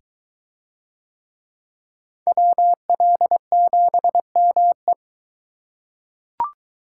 Morse code, heard as a keyed tone, about 22 wpm, 700 Hz. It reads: WL7ME